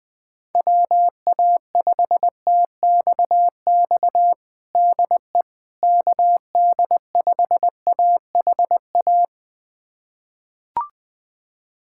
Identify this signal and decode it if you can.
Morse code, heard as a keyed tone, about 20 wpm, 700 Hz. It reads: WA5TXX DE KD5AHA
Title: USA Callsign DE Callsign - 6 Characters - 20wpm